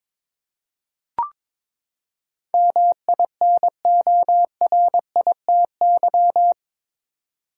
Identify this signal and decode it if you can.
Morse code, heard as a keyed tone, about 22 wpm, 700 Hz. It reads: MINORITY
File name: practice / 22wpm / Eight Letter Words - 22wpm